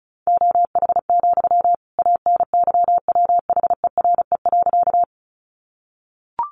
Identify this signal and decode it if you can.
Morse code, heard as a keyed tone, about 34 wpm, 700 Hz. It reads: OH, ANYWHERE.
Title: Sentences from Top 1400 Words 35wpm